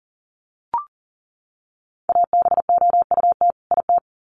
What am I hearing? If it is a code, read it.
Morse code, heard as a keyed tone, about 40 wpm, 700 Hz. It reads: ABOUT IT
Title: Sets of 2 Words 40wpm